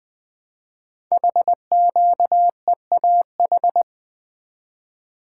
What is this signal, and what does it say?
Morse code, heard as a keyed tone, about 20 wpm, 700 Hz. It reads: HQEAH
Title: Code Groups - 5 Letters - 20wpm